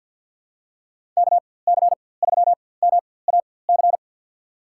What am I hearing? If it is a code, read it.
Morse code, heard as a keyed tone, about 53 wpm, spaced as for 24 wpm, 700 Hz. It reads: KC3MAC
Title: USA Call Signs - 6 Characters - 4x Character Spacing - 50wpm